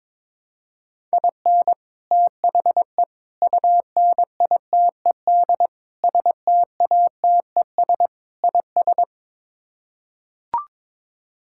Morse code, heard as a keyed tone, about 22 wpm, 700 Hz. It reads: IN THE UNITED STATES IS
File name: practice / 22wpm / Build A Phrase - 5-Word Phrases - 22wpm